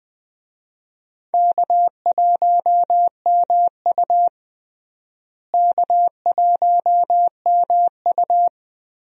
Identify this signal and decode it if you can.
Morse code, heard as a keyed tone, about 21 wpm, 700 Hz. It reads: K1MU K1MU
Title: USA Call Signs - 4 Characters - Repeated 3x - 20wpm